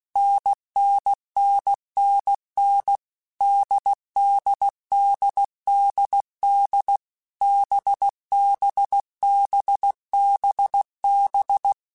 Morse code, heard as a keyed tone, about 16 wpm, 800 Hz. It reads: NNNNN DDDDD BBBBB
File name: Short Runs / Short Runs-mp3s / NDB NDB6 @15